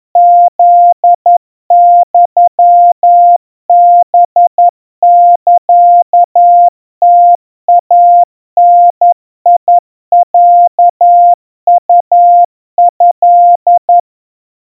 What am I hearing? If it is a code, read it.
Morse code (Japanese Wabun code), heard as a keyed tone, about 11 wpm, 700 Hz. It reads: フユハサムイダロウト
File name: kawa / Morse sample / kamyu 40 cpm 700 Hz Crispy